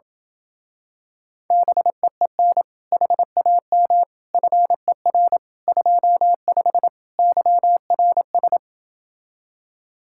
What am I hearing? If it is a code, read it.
Morse code, heard as a keyed tone, about 28 wpm, 700 Hz. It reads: BEEN HAM FER 25 YRS